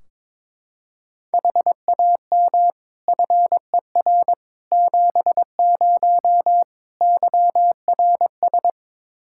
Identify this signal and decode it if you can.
Morse code, heard as a keyed tone, about 22 wpm, 700 Hz. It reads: HAM FER 70 YRS
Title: Practice QSO - Exchange 3 - More Info - 22wpm